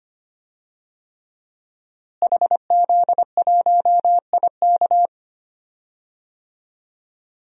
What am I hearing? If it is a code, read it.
Morse code, heard as a keyed tone, about 25 wpm, 700 Hz. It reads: HZ1IK
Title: DXCC-100 Call Signs - 5 Characters - Repeated 3x - 25wpm